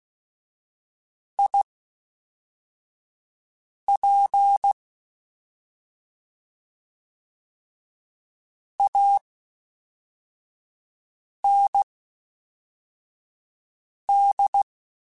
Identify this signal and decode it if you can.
Morse code, heard as a keyed tone, about 16 wpm, spaced as for 4 wpm, 800 Hz. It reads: IP AND
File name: Misc / Misc-mp3s / WordPairs sp @5